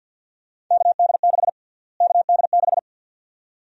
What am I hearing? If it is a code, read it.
Morse code, heard as a keyed tone, about 51 wpm, 700 Hz. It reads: KD6 KD6
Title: USA Call Sign Prefixes - Repeated 2x - 50wpm